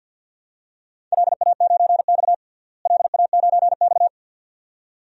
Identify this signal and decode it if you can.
Morse code, heard as a keyed tone, about 52 wpm, 700 Hz. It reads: LA9X LA9X
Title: DXCC-100 Call Signs - 4 Characters - Repeated 2x - 50wpm